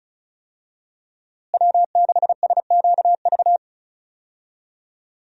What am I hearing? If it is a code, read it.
Morse code, heard as a keyed tone, about 36 wpm, 700 Hz. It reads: W6SQV
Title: USA Call Signs - 5 Characters - Repeated 3x - 35wpm